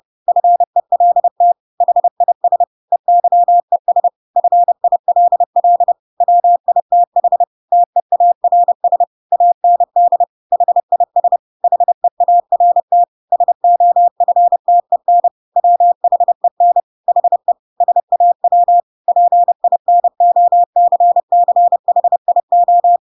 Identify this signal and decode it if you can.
Morse code, heard as a keyed tone, about 30 wpm, 700 Hz. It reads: FELT HIS EYES FILL WITH TEARS AND HIS HEART SOFTEN WHEN HE SAW PINOCCHIO
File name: Pinocchio / Pinocchio ch8 30 30 ik1hge